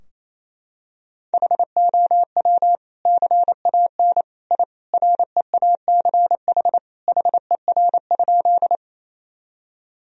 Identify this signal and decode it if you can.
Morse code, heard as a keyed tone, about 28 wpm, 700 Hz. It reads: HOW CAN I REACH HER?
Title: Sentences from Top 1000 Words - Review Part 2 - 28wpm